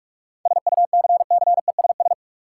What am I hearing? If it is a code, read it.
Morse code, heard as a keyed tone, about 46 wpm, 700 Hz. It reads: SUCCESS
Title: Sentences from Top 900 Words - New Word Drill - 45wpm